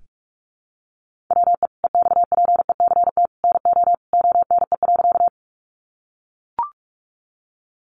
Morse code, heard as a keyed tone, about 44 wpm, 700 Hz. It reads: WE EXPECT NO ONE.
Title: Sentences from Top 1100 Words 45wpm